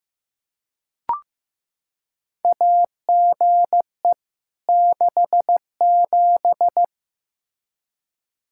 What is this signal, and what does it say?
Morse code, heard as a keyed tone, about 15 wpm, 700 Hz. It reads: AGE 67
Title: K1USN SST QSO Practice 15wpm